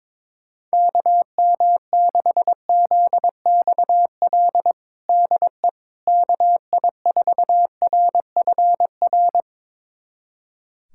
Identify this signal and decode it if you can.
Morse code, heard as a keyed tone, about 22 wpm, 700 Hz. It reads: KM6ZXL DE KI4RFR
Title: USA Callsign DE Callsign 22wpm